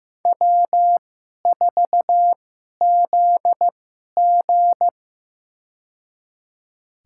Morse code, heard as a keyed tone, about 15 wpm, spaced as for 11 wpm, 700 Hz. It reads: W4ZG
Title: USA Call Signs - 4 Characters - 2x Character Spacing - 15wpm